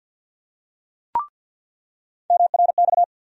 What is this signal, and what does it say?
Morse code, heard as a keyed tone, about 49 wpm, 700 Hz. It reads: MRX
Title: USA Call Sign Postfixes - 3 Letters 50wpm